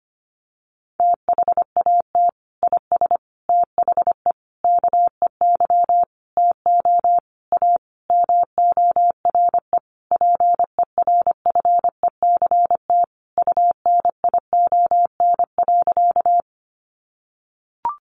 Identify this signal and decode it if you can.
Morse code, heard as a keyed tone, about 25 wpm, 700 Hz. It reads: THAT IS THE KEY TO A MORE PERFECT UNION.
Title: Sentences from Top 1400 Words 25wpm